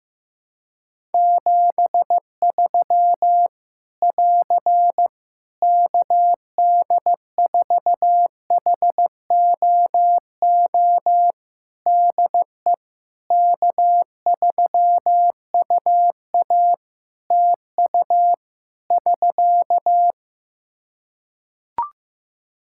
Morse code, heard as a keyed tone, about 15 wpm, 700 Hz. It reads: 73 <AR> KD4HOO DE K3UA TU <SK>
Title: Practice QSO - Exchange 4 - Ending - 15wpm